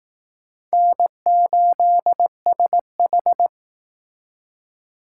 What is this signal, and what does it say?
Morse code, heard as a keyed tone, about 18 wpm, 700 Hz. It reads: N8SH
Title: Speed Racing - USA 1x2 Call Signs 15wpm